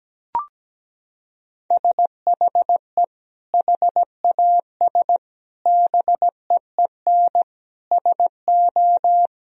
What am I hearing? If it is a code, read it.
Morse code, heard as a keyed tone, about 17 wpm, 700 Hz. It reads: SHE HAS BEEN SO
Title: Sentences from Top 1000 Words - Review Part 2 - 17wpm